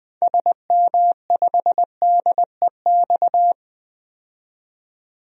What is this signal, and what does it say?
Morse code, heard as a keyed tone, about 20 wpm, 700 Hz. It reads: SM5DEX
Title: DXCC-100 Call Signs - 6 Characters - 20wpm